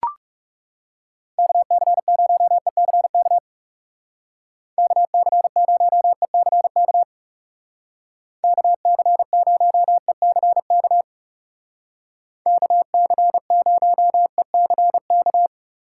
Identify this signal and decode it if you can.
Morse code, heard as a keyed tone, about 37 wpm, 700 Hz. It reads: KC0ECK KC0ECK KC0ECK KC0ECK
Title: Speed Racing - USA 2x3 Call Signs 30wpm